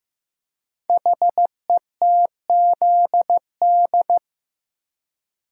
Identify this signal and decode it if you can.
Morse code, heard as a keyed tone, about 15 wpm, 700 Hz. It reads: HETZD
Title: Code Groups - 5 Letters - 15wpm